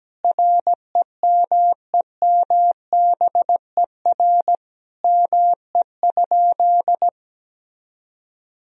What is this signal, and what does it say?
Morse code, heard as a keyed tone, about 17 wpm, 700 Hz. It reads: REMEMBER ME?
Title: Sentences from Top 400 Words and Abbreviations 17wpm